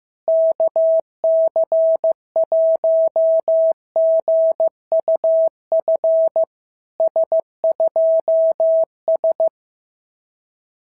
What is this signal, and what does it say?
Morse code, heard as a keyed tone, about 14 wpm, 650 Hz. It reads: KC1GUF S2S
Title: SOTA QSO Practice 15wpm